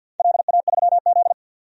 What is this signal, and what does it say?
Morse code, heard as a keyed tone, about 48 wpm, 700 Hz. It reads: LA3Z
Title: DXCC-100 Call Signs - 4 Characters - 50wpm